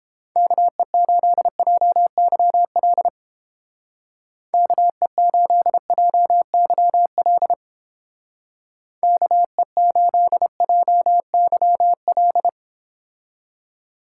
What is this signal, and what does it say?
Morse code, heard as a keyed tone, about 31 wpm, 700 Hz. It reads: KE8JYL KE8JYL KE8JYL